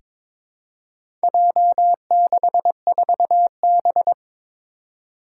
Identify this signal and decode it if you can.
Morse code, heard as a keyed tone, about 22 wpm, 700 Hz. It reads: J64B
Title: Four Challenging Letter-Number Combinations 22wpm